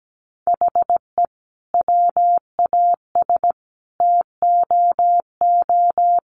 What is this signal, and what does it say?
Morse code, heard as a keyed tone, about 17 wpm, 700 Hz. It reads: HE WAS TOO
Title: Sentences from Top 600 Words 17wpm